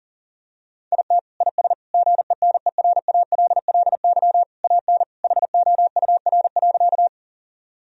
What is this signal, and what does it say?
Morse code, heard as a keyed tone, about 40 wpm, 700 Hz. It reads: IT IS GENERALLY AN HOUR.
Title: Sentences from Top 1000 Words - Review Part 4 - 40wpm